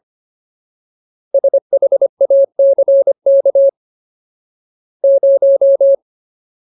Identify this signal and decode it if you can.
Morse code, heard as a keyed tone, about 25 wpm, 550 Hz. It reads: SHACK 0
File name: Ultimate Word List / 25WPM Ultimate